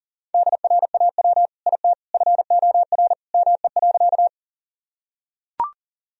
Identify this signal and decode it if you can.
Morse code, heard as a keyed tone, about 40 wpm, 700 Hz. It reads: DRAW IT FOR ME.
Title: Sentences from Top 1000 Words - Review Part 1 - 40wpm